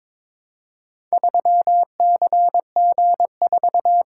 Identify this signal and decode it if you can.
Morse code, heard as a keyed tone, about 22 wpm, 700 Hz. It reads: 3CG4
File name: practice / 22wpm / Four Letter-Number Combinations - 22wpm